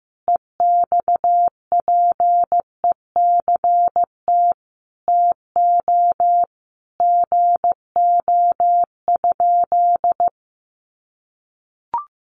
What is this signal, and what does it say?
Morse code, heard as a keyed tone, about 15 wpm, 700 Hz. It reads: EXPECT TO GO?
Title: Sentences from Top 1100 Words 15wpm